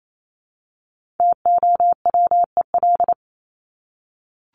Morse code, heard as a keyed tone, about 28 wpm, 700 Hz. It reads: TOWEL
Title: Top 1501-1750 Words 28wpm